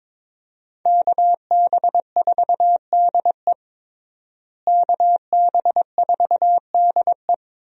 Morse code, heard as a keyed tone, about 21 wpm, 700 Hz. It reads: KB4DE KB4DE
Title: USA Call Signs - 5 Characters - Repeated 3x - 22wpm